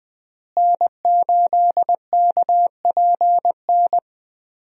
Morse code, heard as a keyed tone, about 21 wpm, 700 Hz. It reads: N8KPN
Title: USA Call Signs - 5 Characters - Repeated 3x - 20wpm